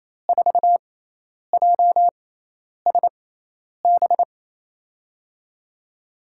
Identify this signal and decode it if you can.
Morse code, heard as a keyed tone, about 28 wpm, spaced as for 10 wpm, 700 Hz. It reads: 4JSB
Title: USA Call Signs - 6 Characters - 6x Character Spacing - 28wpm